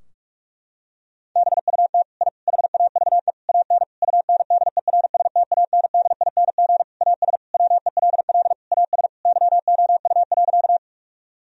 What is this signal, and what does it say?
Morse code, heard as a keyed tone, about 46 wpm, 700 Hz. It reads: BUT I HAVE AN UNDERSTANDING AS WELL AS YOU.
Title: Sentences from Top 1000 Words - Review Part 3 - 45wpm